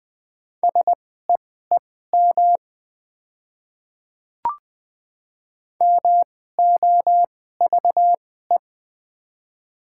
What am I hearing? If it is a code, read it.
Morse code, heard as a keyed tone, about 21 wpm, spaced as for 15 wpm, 700 Hz. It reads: SEEM MOVE
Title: 2x Character Spacing - Top 300 Words - 20wpm